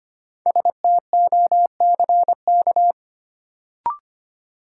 Morse code, heard as a keyed tone, about 25 wpm, 700 Hz. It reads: STOCK
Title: Sets of 5 Words - New Word Drill - 25wpm